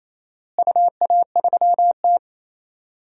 Morse code, heard as a keyed tone, about 28 wpm, 700 Hz. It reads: UA3T
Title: DXCC-100 Call Signs - 4 Characters - Repeated 3x - 28wpm